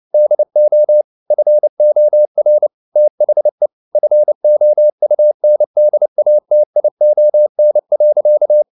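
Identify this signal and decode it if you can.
Morse code, heard as a keyed tone, about 29 wpm, 600 Hz. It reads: DO FOR THE FOUNDATION.
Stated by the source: Morse code at 29 wpm, sending Ken Follett's novel Never in English, from CW med Gnister.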